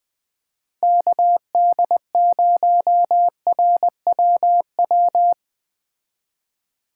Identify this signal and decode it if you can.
Morse code, heard as a keyed tone, about 21 wpm, 700 Hz. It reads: KD0RWW
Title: Speed Racing - USA 2x3 Call Signs 17wpm